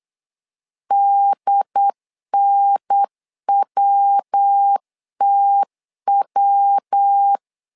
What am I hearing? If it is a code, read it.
Morse code, heard as a keyed tone, about 8 wpm, 800 Hz. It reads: DNWTW
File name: LEZIONI / 3 MWRO / 3 MW 8wpm 2501